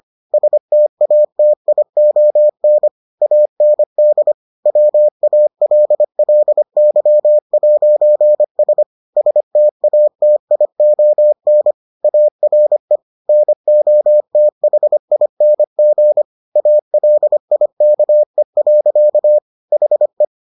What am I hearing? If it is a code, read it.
Morse code, heard as a keyed tone, about 25 wpm, 600 Hz. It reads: STATION AND WALLY'S STATION ARE NOTHING ALIKE. HE